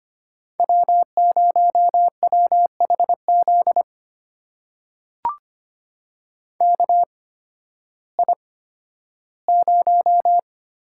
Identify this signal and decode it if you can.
Morse code, heard as a keyed tone, about 25 wpm, spaced as for 13 wpm, 700 Hz. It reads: W0WHZ KI0
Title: USA Call Signs - 5 Characters - 8x Character Spacing - 25wpm